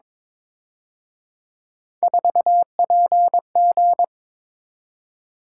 Morse code, heard as a keyed tone, about 27 wpm, 700 Hz. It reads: 4PG
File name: Speed-Racing - Three Letter-Number Combinations - 22wpm